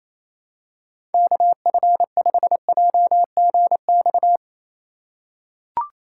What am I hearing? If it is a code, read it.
Morse code, heard as a keyed tone, about 28 wpm, 700 Hz. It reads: KF5JGX
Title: USA Call Signs - 6 Characters - 4x Character Spacing - 28wpm